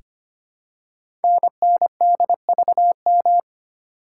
Morse code, heard as a keyed tone, about 25 wpm, 700 Hz. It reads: NNDVM
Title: Code Groups - 5 Letters - 25wpm